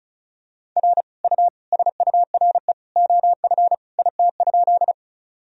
Morse code, heard as a keyed tone, about 36 wpm, 700 Hz. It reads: R U SURE OF IT?
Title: Sentences from Top 300 Words and Abbreviations 35wpm